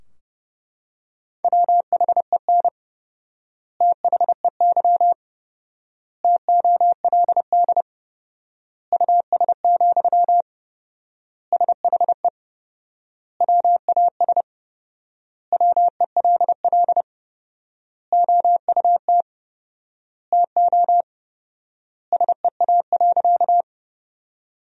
Morse code, heard as a keyed tone, about 32 wpm, 700 Hz. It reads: WHEN THEY TOLD US, SHE WAS WELL OUT TO SEA.
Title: 4x Word-Spacing - Sentences from Top 400 Words - 30wpm